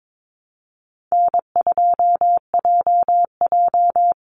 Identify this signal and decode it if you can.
Morse code, heard as a keyed tone, about 22 wpm, 700 Hz. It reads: N2JJ
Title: Easy Call Signs Repeated 2x - 22wpm